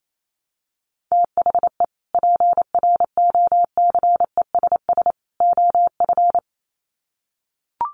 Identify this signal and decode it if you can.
Morse code, heard as a keyed tone, about 28 wpm, 700 Hz. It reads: THE PROCESS OF
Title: Sets of 3 Words 28wpm